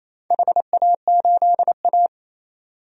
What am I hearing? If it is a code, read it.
Morse code, heard as a keyed tone, about 27 wpm, 700 Hz. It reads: HA8A
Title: DXCC-100 Call Signs - 4 Characters - 28wpm